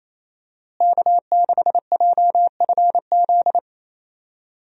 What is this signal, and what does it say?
Morse code, heard as a keyed tone, about 28 wpm, 700 Hz. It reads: K6JFZ
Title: USA Call Signs - 5 Characters - Repeated 2x - 28wpm